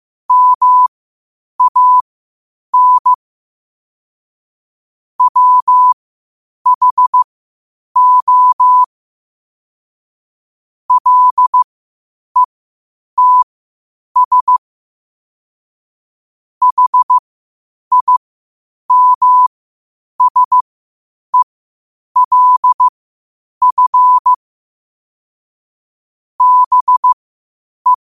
Morse code, heard as a keyed tone, about 15 wpm, spaced as for 8 wpm, 1000 Hz. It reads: MAN WHO LETS HIMSELF BE
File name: qotdfiles / 8wpm